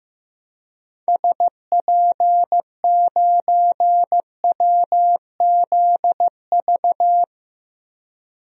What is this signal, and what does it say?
Morse code, heard as a keyed tone, about 15 wpm, 700 Hz. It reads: SP9WZV
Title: DXCC-100 Call Signs - 6 Characters - Repeated 2x - 15wpm